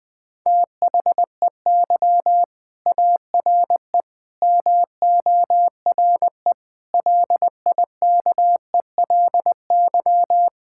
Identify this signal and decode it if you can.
Morse code, heard as a keyed tone, about 20 wpm, 700 Hz. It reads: THEY ARE MORE LIKELY
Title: Sentences with 4-Word Phrases - 20wpm